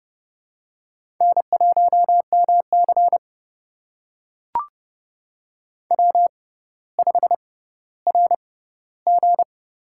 Morse code, heard as a keyed tone, about 32 wpm, spaced as for 16 wpm, 700 Hz. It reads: N1MC W5RG
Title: USA Call Signs - 4 Characters - 6x Character Spacing - 30wpm